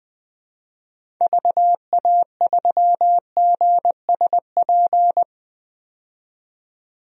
Morse code, heard as a keyed tone, about 20 wpm, 700 Hz. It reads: VA3GSP
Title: DXCC-100 Call Signs - 6 Characters - 20wpm